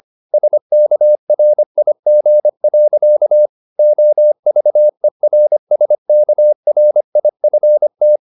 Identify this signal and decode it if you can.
Morse code, heard as a keyed tone, about 25 wpm, 600 Hz. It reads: SKRIG. OVERSKRIFT
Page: Ferie 25 wpm